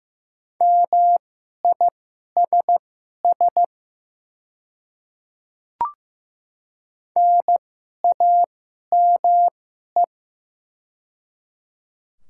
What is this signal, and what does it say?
Morse code, heard as a keyed tone, about 15 wpm, spaced as for 11 wpm, 700 Hz. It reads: MISS NAME
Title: 2x Character Spacing - Top 500 Words - 15wpm